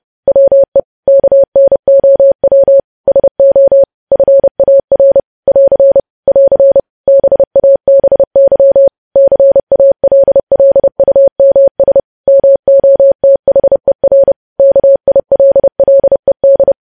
Morse code, heard as a keyed tone, about 30 wpm, 550 Hz. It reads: WE KNOW SO FAR <AR> <AR> BABY CALLUMS MOTHER KILLED
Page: news headlines in morse code at 30 wpm